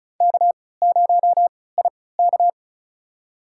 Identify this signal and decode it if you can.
Morse code, heard as a keyed tone, about 34 wpm, spaced as for 20 wpm, 700 Hz. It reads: K0IK
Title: USA Call Signs - 3x Character Spacing - 35wpm